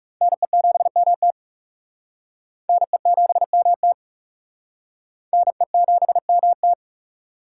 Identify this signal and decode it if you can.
Morse code, heard as a keyed tone, about 38 wpm, 700 Hz. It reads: NE7MT NE7MT NE7MT